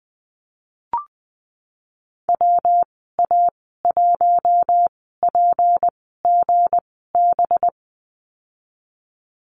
Morse code, heard as a keyed tone, about 20 wpm, spaced as for 15 wpm, 700 Hz. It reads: WA1PGB
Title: 193 Lesson 31g - Letter v - Call Signs 20wpm